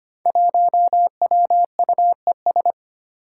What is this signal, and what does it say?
Morse code, heard as a keyed tone, about 25 wpm, 700 Hz. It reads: 1WUES
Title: Five Challenging Letter-Number Combinations 25wpm